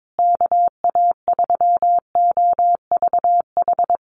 Morse code, heard as a keyed tone, about 22 wpm, 700 Hz. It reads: KA3OVH
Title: USA Call Signs Repeated 2x 22wpm